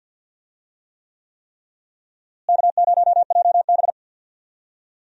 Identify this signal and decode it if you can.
Morse code, heard as a keyed tone, about 47 wpm, 700 Hz. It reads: K0JB
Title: USA Call Signs - 4 Characters - Repeated 3x - 50wpm